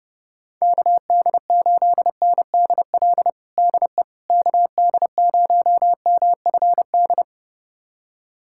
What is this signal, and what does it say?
Morse code, heard as a keyed tone, about 32 wpm, 700 Hz. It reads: KD8NDL DE KD0MFD
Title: USA Callsign DE Callsign 30wpm